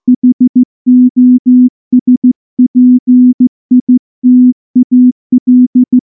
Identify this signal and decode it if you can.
Morse code, heard as a keyed tone, about 14 wpm, 250 Hz. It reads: HOSPITAL
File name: morse / morse-training-15WPM